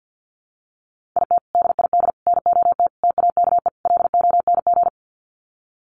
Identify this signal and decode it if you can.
Morse code, heard as a keyed tone, about 48 wpm, 700 Hz. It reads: IT DID NOT TAKE LONG
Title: Sets of 5 Words Encore2 50wpm